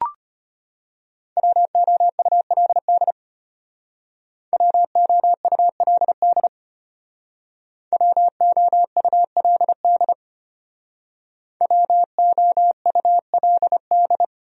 Morse code, heard as a keyed tone, about 32 wpm, 700 Hz. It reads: WOULD WOULD WOULD WOULD